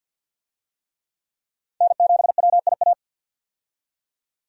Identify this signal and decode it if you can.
Morse code, heard as a keyed tone, about 53 wpm, 700 Hz. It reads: N7WIA
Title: USA Call Signs - 5 Characters - 50wpm